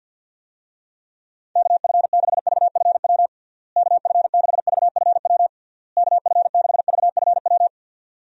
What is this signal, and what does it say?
Morse code, heard as a keyed tone, about 50 wpm, 700 Hz. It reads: KF6VFW KF6VFW KF6VFW
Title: USA Call Signs - 6 Characters - Repeated 3x - 50wpm